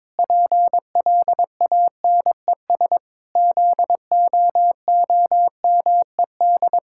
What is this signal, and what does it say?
Morse code, heard as a keyed tone, about 22 wpm, 700 Hz. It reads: PLANES ZOOMED
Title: Pangram Challenge v2 - 22wpm